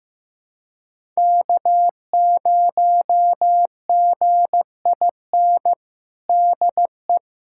Morse code, heard as a keyed tone, about 15 wpm, 700 Hz. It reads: K0GIN DE
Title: USA Callsign DE Callsign - 5 Characters - 15wpm